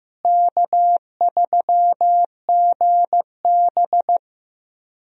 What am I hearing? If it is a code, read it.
Morse code, heard as a keyed tone, about 15 wpm, 700 Hz. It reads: K3GB
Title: USA Call Signs - 4 Characters - Repeated 3x - 15wpm